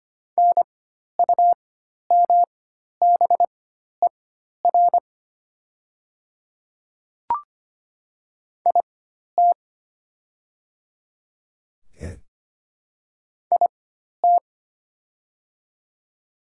Morse code, heard as a keyed tone, about 25 wpm, spaced as for 12 wpm, 700 Hz. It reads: NUMBER IT IT
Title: 4x Character Spacing - Top 100 Words - 25wpm